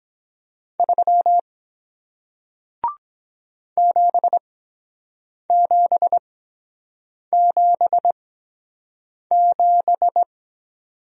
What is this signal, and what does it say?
Morse code, heard as a keyed tone, about 23 wpm, spaced as for 7 wpm, 700 Hz. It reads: 3 7777